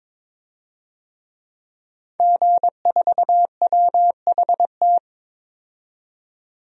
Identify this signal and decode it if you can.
Morse code, heard as a keyed tone, about 22 wpm, 700 Hz. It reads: G4WHT
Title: DXCC-100 Call Signs - 5 Characters - 22wpm